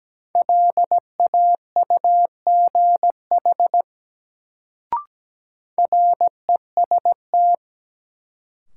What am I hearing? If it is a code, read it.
Morse code, heard as a keyed tone, about 17 wpm, 700 Hz. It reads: LAUGH REST
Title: Sentences from Top 500 Words - New Word Drill - 17wpm